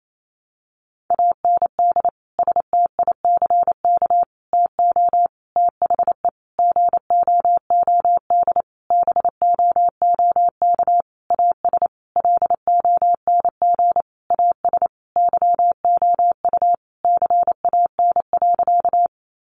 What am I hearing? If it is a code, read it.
Morse code, heard as a keyed tone, about 28 wpm, 700 Hz. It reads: AND STICK TO THE GOOD BOOK AS LONG AS YOU CAN.
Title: Sentences from Top 1200 Words 28wpm